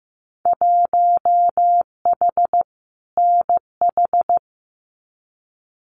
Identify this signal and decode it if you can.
Morse code, heard as a keyed tone, about 15 wpm, 700 Hz. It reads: 1H NH
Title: Winter Field Day Exchanges 15wpm